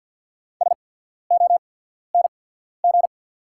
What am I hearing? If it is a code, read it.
Morse code, heard as a keyed tone, about 48 wpm, spaced as for 14 wpm, 690 Hz. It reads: SONG
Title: 8x Character Spacing - Top 500 Words - 50wpm